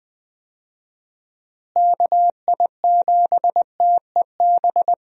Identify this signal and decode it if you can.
Morse code, heard as a keyed tone, about 20 wpm, 700 Hz. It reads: KI7TEB
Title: USA Call Signs - 6 Characters - Repeated 3x - 20wpm